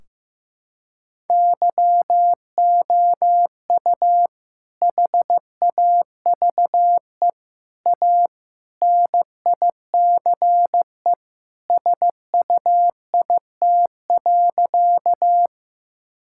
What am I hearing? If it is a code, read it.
Morse code, heard as a keyed tone, about 15 wpm, 700 Hz. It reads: YOU HAVE A NICE SUIT.
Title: Sentences from Top 1600 Words 15wpm